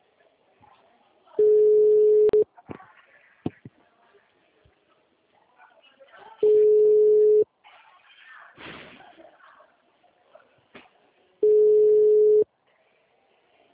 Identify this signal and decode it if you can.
Ringback tone (Europe, 425 Hz), with speech.
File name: out- -2006- - - .49